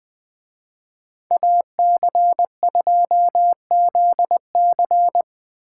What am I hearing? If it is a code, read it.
Morse code, heard as a keyed tone, about 21 wpm, 700 Hz. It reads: AC2ZC
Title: Speed Racing - USA Call Signs 17wpm